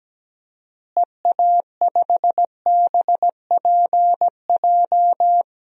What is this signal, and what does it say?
Morse code, heard as a keyed tone, about 17 wpm, 700 Hz. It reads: EA5BPJ
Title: DXCC-100 Call Signs - 6 Characters - Repeated 3x - 17wpm